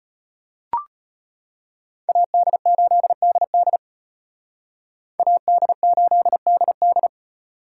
Morse code, heard as a keyed tone, about 33 wpm, 700 Hz. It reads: AD8DD AD8DD